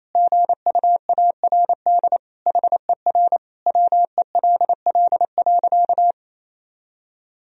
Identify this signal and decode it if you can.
Morse code, heard as a keyed tone, about 27 wpm, 700 Hz. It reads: GUARD HER WELL.
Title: Sentences from Top 1000 Words - Review Part 4 - 28wpm